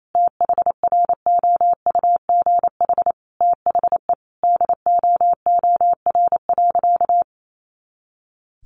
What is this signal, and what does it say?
Morse code, heard as a keyed tone, about 28 wpm, 700 Hz. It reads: THROUGH THE DOOR.
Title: Sentences from Top 400 Words 28wpm